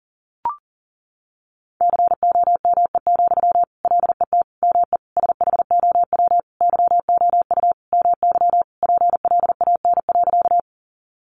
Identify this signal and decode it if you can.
Morse code, heard as a keyed tone, about 40 wpm, 700 Hz. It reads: COME, LET ME SHOW YOU MY PLAN.
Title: Sentences from Top 500 Words 40wpm